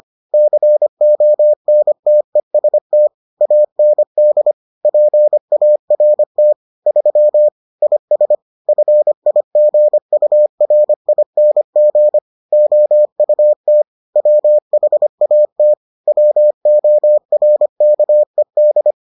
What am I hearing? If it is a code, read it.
Morse code, heard as a keyed tone, about 25 wpm, 600 Hz. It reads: CONTEST AND PART 3 IS FIGURING OUT WHAT WORKED